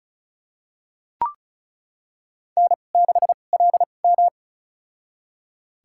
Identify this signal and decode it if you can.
Morse code, heard as a keyed tone, about 36 wpm, spaced as for 25 wpm, 700 Hz. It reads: N6LM
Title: USA Call Signs - 4 Characters - 2x Character Spacing - 35wpm